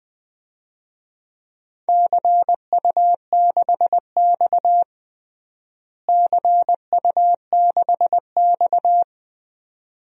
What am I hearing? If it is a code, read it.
Morse code, heard as a keyed tone, about 20 wpm, 700 Hz. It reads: CU6X CU6X
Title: DXCC-100 Call Signs - 4 Characters - Repeated 2x - 20wpm